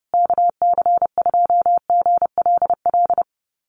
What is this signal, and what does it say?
Morse code, heard as a keyed tone, about 30 wpm, 700 Hz. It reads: KC2GLL
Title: USA Call Signs Repeated 3x 30wpm